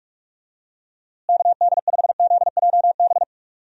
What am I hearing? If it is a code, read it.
Morse code, heard as a keyed tone, about 44 wpm, 700 Hz. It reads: KD5ZJB
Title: USA Call Signs - 6 Characters - Repeated 3x - 45wpm